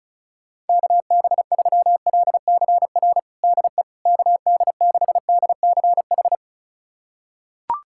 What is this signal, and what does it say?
Morse code, heard as a keyed tone, about 36 wpm, 700 Hz. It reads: KB3LCR DE KD6DCH
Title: USA Callsign DE Callsign - 6 Characters - 35wpm